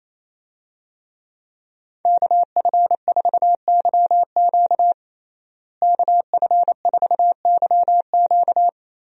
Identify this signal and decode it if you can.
Morse code, heard as a keyed tone, about 28 wpm, 700 Hz. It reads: KF4YQ KF4YQ
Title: USA Call Signs - 5 Characters - Repeated 2x - 28wpm